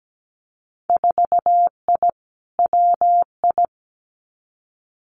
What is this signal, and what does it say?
Morse code, heard as a keyed tone, about 17 wpm, 700 Hz. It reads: 4I WI
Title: Winter Field Day Exchanges 17wpm